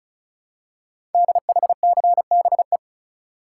Morse code, heard as a keyed tone, about 36 wpm, 700 Hz. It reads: DHCBE
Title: Code Groups - 5 Letters - 35wpm